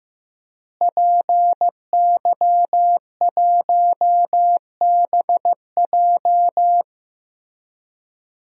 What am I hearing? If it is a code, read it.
Morse code, heard as a keyed tone, about 15 wpm, 700 Hz. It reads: PY1BJ
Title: DXCC-100 Call Signs - 5 Characters - Repeated 3x - 15wpm